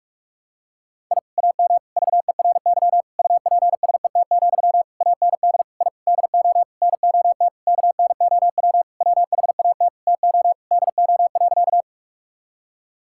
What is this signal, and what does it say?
Morse code, heard as a keyed tone, about 46 wpm, 700 Hz. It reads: I AM VERY UPSET, AND I DO NOT KNOW WHAT TO DO.
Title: Sentences from Top 1000 Words - Review Part 5 - 45wpm